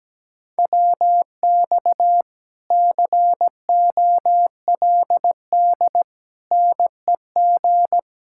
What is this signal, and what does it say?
Morse code, heard as a keyed tone, about 17 wpm, 700 Hz. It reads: WX COLD NEG
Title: Speed-Racing Common QSO Phrases - 17wpm